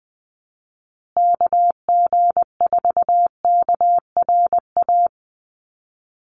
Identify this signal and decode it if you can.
Morse code, heard as a keyed tone, about 20 wpm, 700 Hz. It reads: KG4KRA
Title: USA Call Signs Repeated 2x 20wpm